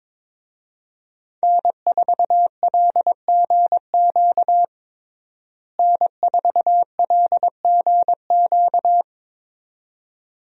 Morse code, heard as a keyed tone, about 22 wpm, 700 Hz. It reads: N4LGQ N4LGQ
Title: USA Call Signs - 5 Characters - Repeated 2x - 22wpm